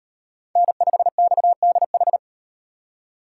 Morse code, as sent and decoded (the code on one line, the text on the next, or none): -. ..... -..- -.. ....
N5XDH